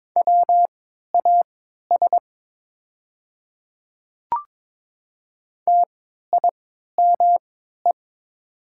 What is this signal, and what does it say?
Morse code, heard as a keyed tone, about 23 wpm, spaced as for 13 wpm, 700 Hz. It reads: WAS TIME
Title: 3x Character Spacing - Top 100 Words - 22wpm